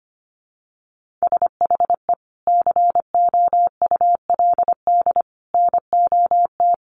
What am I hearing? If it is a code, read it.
Morse code, heard as a keyed tone, about 25 wpm, 700 Hz. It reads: SHE COULD NOT
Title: Sentences from Top 1400 Words 25wpm